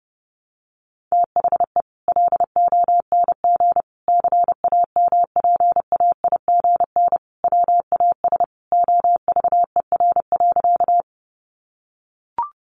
Morse code, heard as a keyed tone, about 30 wpm, 700 Hz. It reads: THE LONG CAMPAIGN WAS OVER.
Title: Sentences from Top 1200 Words 30wpm